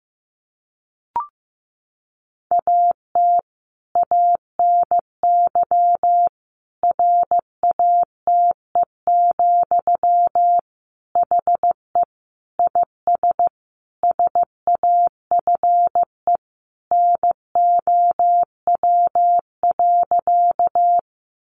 Morse code, heard as a keyed tone, about 15 wpm, 700 Hz. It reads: AT ANY RATE, HE IS SAFE NOW.
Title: Sentences from Top 900 Words 15wpm